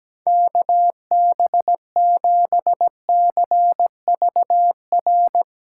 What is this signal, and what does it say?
Morse code, heard as a keyed tone, about 17 wpm, 700 Hz. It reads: KB7CVR
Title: USA Call Signs - 6 Characters - 6x Character Spacing - 17wpm